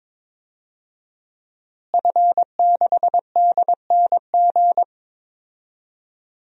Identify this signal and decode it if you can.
Morse code, heard as a keyed tone, about 22 wpm, 700 Hz. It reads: F6DNG
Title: DXCC-100 Call Signs - 5 Characters - Repeated 3x - 22wpm